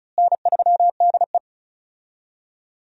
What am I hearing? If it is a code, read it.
Morse code, heard as a keyed tone, about 36 wpm, 700 Hz. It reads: N3DE
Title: USA Call Signs - 4 Characters - Repeated 3x - 35wpm